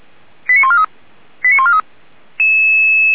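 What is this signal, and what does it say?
Five-tone selective call (ZVEI), heard as quick runs of five tones, tones of about 70 ms. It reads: 89143 89143
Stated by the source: ZVEI five-tone selective call received on SDR.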